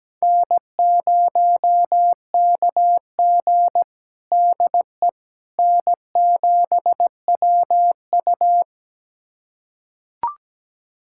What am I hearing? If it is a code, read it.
Morse code, heard as a keyed tone, about 17 wpm, 700 Hz. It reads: N0KG DE N7WU
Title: USA Callsign DE Callsign - 4 Characters - 17wpm